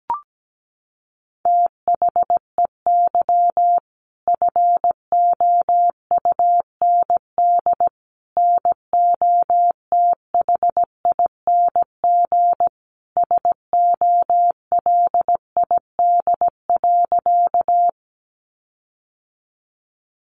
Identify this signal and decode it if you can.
Morse code, heard as a keyed tone, about 17 wpm, 700 Hz. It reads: THEY FOUND NOTHING SOLID.
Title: Sentences from Top 1500 Words 17wpm